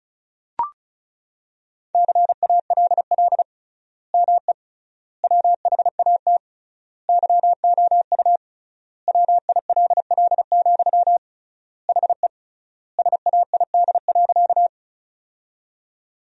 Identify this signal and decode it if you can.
Morse code, heard as a keyed tone, about 34 wpm, 700 Hz. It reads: CALL ME WHAT YOU WILL, HE SAID.
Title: 3x Word-Spacing - Sentences from Top 200 Words - 35wpm